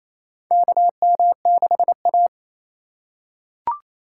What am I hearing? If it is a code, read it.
Morse code, heard as a keyed tone, about 28 wpm, 700 Hz. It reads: KM6A
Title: USA Call Signs - 4 Characters - 2x Character Spacing - 28wpm